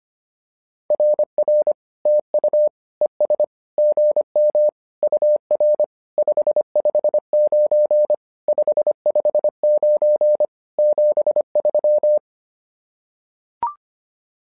Morse code, heard as a keyed tone, about 25 wpm, 600 Hz. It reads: RR TU ES GM UR 559 559 73
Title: SOTA QSO Practice 25wpm